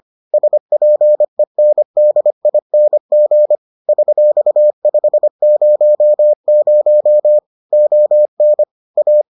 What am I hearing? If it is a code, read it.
Morse code, heard as a keyed tone, about 25 wpm, 600 Hz. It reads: SPENDING $500 ON A